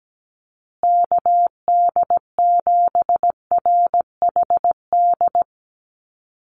Morse code, heard as a keyed tone, about 17 wpm, 700 Hz. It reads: KD7RHD
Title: USA Call Signs Repeated 3x 17wpm